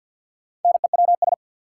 Morse code, heard as a keyed tone, about 53 wpm, 700 Hz. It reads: NEWS
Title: Sentences from Top 800 Words - New Word Drill - 50wpm